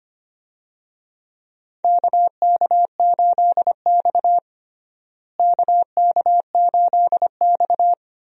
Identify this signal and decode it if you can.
Morse code, heard as a keyed tone, about 25 wpm, 700 Hz. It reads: KK8X KK8X
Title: USA Call Signs - 4 Characters - Repeated 2x - 25wpm